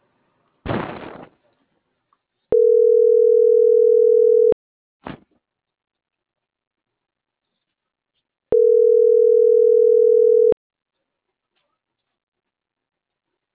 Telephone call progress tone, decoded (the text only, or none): ringback tone